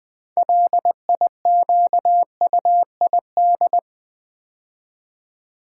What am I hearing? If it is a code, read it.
Morse code, heard as a keyed tone, about 20 wpm, 700 Hz. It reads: LIQUID
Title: 6 Letter Words - First 3 Letters Given - 20wpm